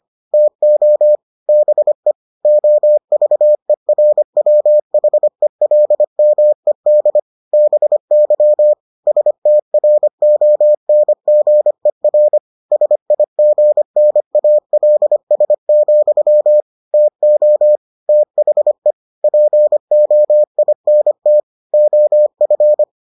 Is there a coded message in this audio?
Yes: Morse code from 0.3 to 22.8 s, heard as a keyed tone, about 25 wpm, 600 Hz. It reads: TO BE OVERWHELMED BY STRONGER SIGNALS, TO THE POINT OF